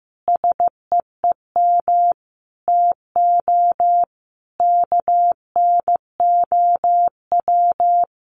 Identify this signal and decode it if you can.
Morse code, heard as a keyed tone, about 15 wpm, 700 Hz. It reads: SEEM TO KNOW
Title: Sentences from Top 1400 Words 15wpm